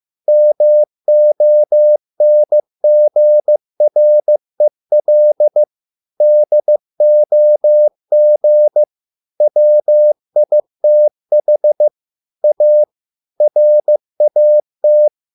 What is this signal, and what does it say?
Morse code, heard as a keyed tone, about 15 wpm, 600 Hz. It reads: MONGREL DOG WITH A RAT